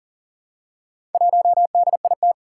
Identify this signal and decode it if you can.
Morse code, heard as a keyed tone, about 40 wpm, 700 Hz. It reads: 1DIT
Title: Four Challenging Letter-Number Combinations 40wpm